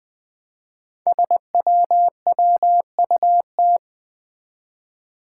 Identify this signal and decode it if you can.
Morse code, heard as a keyed tone, about 20 wpm, 700 Hz. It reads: SWWUT
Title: Five Challenging Letter-Number Combinations 20wpm